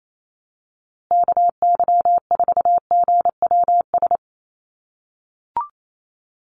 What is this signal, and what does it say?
Morse code, heard as a keyed tone, about 28 wpm, 700 Hz. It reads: KY4GWS
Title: USA Call Signs Repeated 3x 28wpm